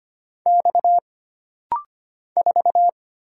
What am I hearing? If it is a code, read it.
Morse code, heard as a keyed tone, about 25 wpm, 700 Hz. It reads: X 4
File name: Single Character (Letter:Number:Punctuation:Prosign) - Rapid-Fire - 25wpm